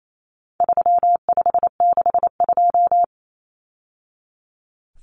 Morse code, heard as a keyed tone, about 28 wpm, 700 Hz. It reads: 3562
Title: Numbers - 4-Digit 28wpm